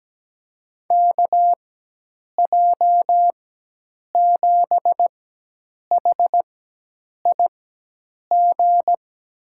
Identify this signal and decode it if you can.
Morse code, heard as a keyed tone, about 17 wpm, spaced as for 8 wpm, 700 Hz. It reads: KJ7HIG
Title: USA Call Signs - 6 Characters - 4x Character Spacing - 17wpm